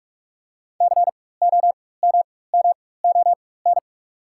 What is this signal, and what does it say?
Morse code, heard as a keyed tone, about 43 wpm, spaced as for 21 wpm, 690 Hz. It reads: COMMON
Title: 4x Character Spacing - Top 500 Words - 45wpm